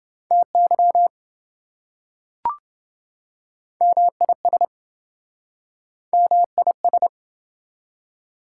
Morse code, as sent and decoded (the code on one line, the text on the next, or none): - -.-- / -- .. ... / -- .. ...
TY MIS MIS